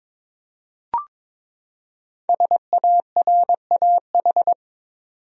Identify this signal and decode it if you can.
Morse code, heard as a keyed tone, about 22 wpm, 700 Hz. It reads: SARAH
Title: USA Common Names - Max 5 Letters - 22wpm